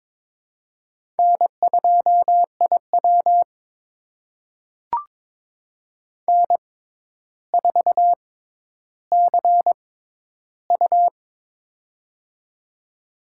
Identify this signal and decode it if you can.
Morse code, heard as a keyed tone, about 22 wpm, spaced as for 11 wpm, 700 Hz. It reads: N2IW N4CU
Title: USA Call Signs - 4 Characters - 6x Character Spacing - 22wpm